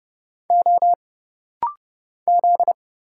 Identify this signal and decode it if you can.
Morse code, heard as a keyed tone, about 30 wpm, 700 Hz. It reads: O Z
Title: Single Letters - Rapid-Fire 30wpm